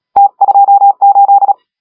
Morse code, heard as a keyed tone, about 36 wpm, 810 Hz. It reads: T28
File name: AUDIO2019 / HG1R CW SSB MIX